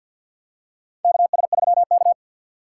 Morse code, heard as a keyed tone, about 53 wpm, 700 Hz. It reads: KS3K
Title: USA Call Signs - 2x Character Spacing - 50wpm